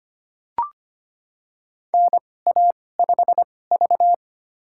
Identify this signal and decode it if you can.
Morse code, heard as a keyed tone, about 25 wpm, spaced as for 18 wpm, 700 Hz. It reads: NA5V
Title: USA Call Signs - 2x Character Spacing - 25wpm